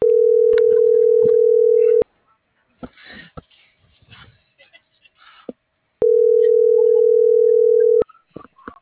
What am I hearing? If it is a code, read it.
Ringback tone (North America), 440 + 480 Hz.